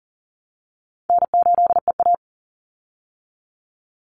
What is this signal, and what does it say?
Morse code, heard as a keyed tone, about 40 wpm, 700 Hz. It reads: N8EA
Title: Call Signs Repeated 3x - 40wpm